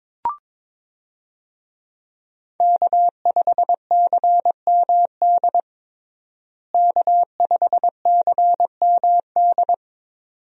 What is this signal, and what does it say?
Morse code, heard as a keyed tone, about 22 wpm, 700 Hz. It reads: K5CMD K5CMD
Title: USA Call Signs - 5 Characters - Repeated 2x - 22wpm